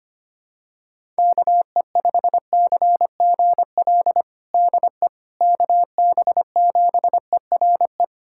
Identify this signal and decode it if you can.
Morse code, heard as a keyed tone, about 25 wpm, 700 Hz. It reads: KE5CGL DE KB7ERE
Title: USA Callsign DE Callsign - 6 Characters - 25wpm